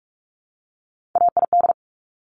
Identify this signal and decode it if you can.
Morse code, heard as a keyed tone, about 46 wpm, 700 Hz. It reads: AID
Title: Top Three Letter Combinations in Words 45wpm